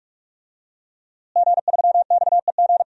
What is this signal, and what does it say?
Morse code, heard as a keyed tone, about 45 wpm, 700 Hz. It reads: G3XEG
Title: DXCC-100 Call Signs - 5 Characters - Repeated 3x - 45wpm